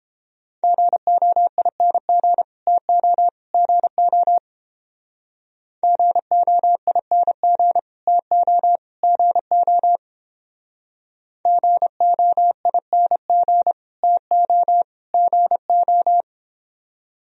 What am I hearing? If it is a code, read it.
Morse code, heard as a keyed tone, about 31 wpm, 700 Hz. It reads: GOING TO GO GOING TO GO GOING TO GO